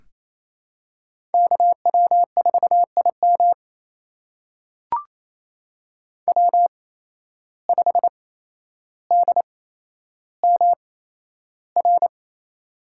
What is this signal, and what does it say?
Morse code, heard as a keyed tone, about 28 wpm, spaced as for 12 wpm, 700 Hz. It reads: KW4IM W5DMR
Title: USA Call Signs - 5 Characters - 8x Character Spacing - 28wpm